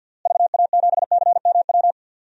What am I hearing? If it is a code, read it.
Morse code, heard as a keyed tone, about 48 wpm, 700 Hz. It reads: VA7CMW
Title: DXCC-100 Call Signs - 6 Characters - Repeated 3x - 50wpm